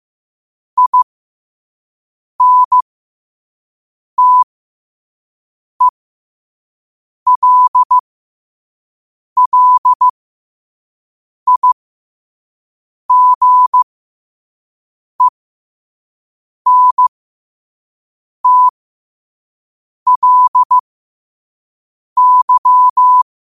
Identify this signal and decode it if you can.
Morse code, heard as a keyed tone, about 15 wpm, spaced as for 5 wpm, 1000 Hz. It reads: INTELLIGENTLY
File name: qotdfiles / 5wpm